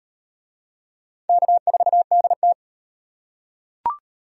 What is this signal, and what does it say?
Morse code, heard as a keyed tone, about 38 wpm, 700 Hz. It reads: K4DT